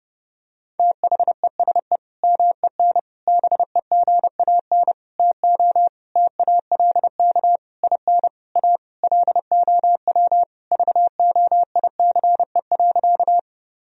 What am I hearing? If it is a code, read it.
Morse code, heard as a keyed tone, about 30 wpm, 700 Hz. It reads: THESE MEN BEGAN TO TALK IN A LOW VOICE.
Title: Sentences from Top 1000 Words - Review Part 1 - 30wpm